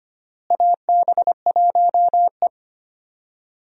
Morse code, heard as a keyed tone, about 25 wpm, 700 Hz. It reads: AB1E
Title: USA Call Signs - 4 Characters - Repeated 3x - 25wpm